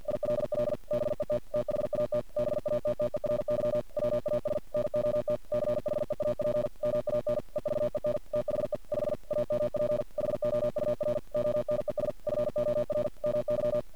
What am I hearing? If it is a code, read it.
Morse code, heard as a keyed tone, about 50 wpm, 600 Hz. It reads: ILL BET THAT BATTERY WAS NOT CHEAP MAN EVEN THE 5 AMP HOUR ONES FOR MY
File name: archived / CRSnet